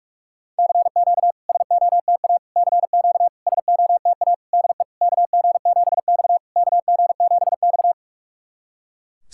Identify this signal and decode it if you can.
Morse code, heard as a keyed tone, about 46 wpm, 690 Hz. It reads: CQ SOTA CQ SOTA DE KG7X KG7X
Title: SOTA QSO Practice 45wpm